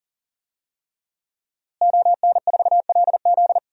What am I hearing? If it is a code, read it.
Morse code, heard as a keyed tone, about 38 wpm, 700 Hz. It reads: ON4LZ
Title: DXCC-100 Call Signs - 5 Characters - Repeated 3x - 40wpm